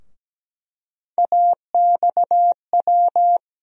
Morse code, heard as a keyed tone, about 17 wpm, 700 Hz. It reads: AXW
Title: USA Call Sign Postfixes - 3 Letters 17wpm